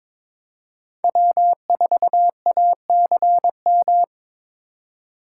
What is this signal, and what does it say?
Morse code, heard as a keyed tone, about 26 wpm, 700 Hz. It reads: W4ACM
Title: Speed Racing - USA 1x3 Call Signs 22wpm